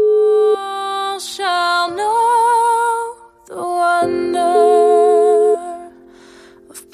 Ringback tone (Europe, 425 Hz).